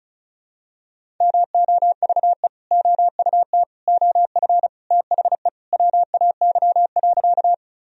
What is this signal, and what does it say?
Morse code, heard as a keyed tone, about 34 wpm, 700 Hz. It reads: MOVE OUT OF THE WAY.
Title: Sentences from Top 1000 Words - Review Part 1 - 35wpm